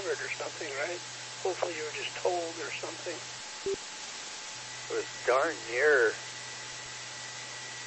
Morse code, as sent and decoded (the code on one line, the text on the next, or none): none